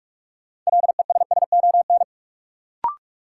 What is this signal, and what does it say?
Morse code, heard as a keyed tone, about 46 wpm, 700 Hz. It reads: LESSON
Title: Sentences from Top 1200 Words - New Word Drill Alt - 45wpm